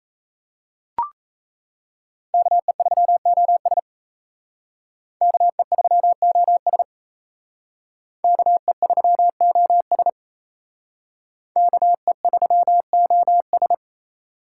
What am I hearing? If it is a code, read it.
Morse code, heard as a keyed tone, about 34 wpm, 700 Hz. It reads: KE3OS KE3OS KE3OS KE3OS
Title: Speed Racing - USA 2x2 Call Signs 28wpm